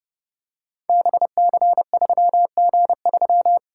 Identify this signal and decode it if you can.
Morse code, heard as a keyed tone, about 30 wpm, 700 Hz. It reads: BC3G3
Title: Code Groups - 5 Letter:Numbers - 30wpm